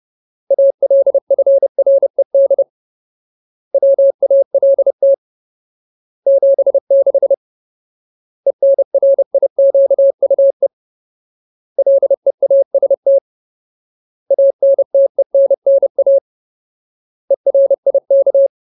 Morse code, heard as a keyed tone, about 30 wpm, 550 Hz. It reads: ALFRED WALT 76 ENRIQUE LEAST ANTENNA ERIK
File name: Ultimate Word List / 30WPM Ultimate